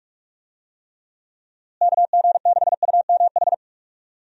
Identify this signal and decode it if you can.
Morse code, heard as a keyed tone, about 46 wpm, 700 Hz. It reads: KG6UMH
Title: USA Call Signs - 6 Characters - Repeated 2x - 45wpm